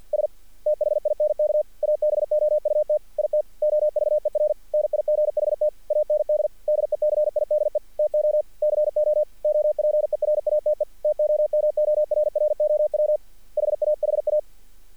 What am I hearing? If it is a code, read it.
Morse code, heard as a keyed tone, about 47 wpm, 600 Hz. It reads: S THINK ABOUT IT OVER NIGHT AND DECIDE TO CO OPERATE TOMORROW HAHA
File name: archived / CRSnet